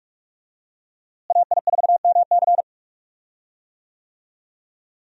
Morse code, heard as a keyed tone, about 46 wpm, 700 Hz. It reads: AI4MC
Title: USA Call Signs - 5 Characters - Repeated 3x - 45wpm